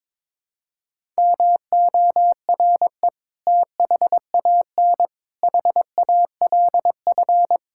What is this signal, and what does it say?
Morse code, heard as a keyed tone, about 22 wpm, 700 Hz. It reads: MORE THAN HALF
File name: practice / 22wpm / Build A Phrase - 5-Word Phrases - 22wpm